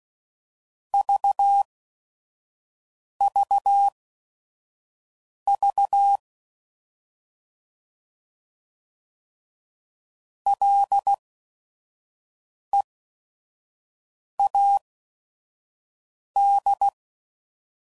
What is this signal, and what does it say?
Morse code, heard as a keyed tone, about 16 wpm, spaced as for 5 wpm, 800 Hz. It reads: VVV LEAD
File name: Prefixes / By First Letter / Ls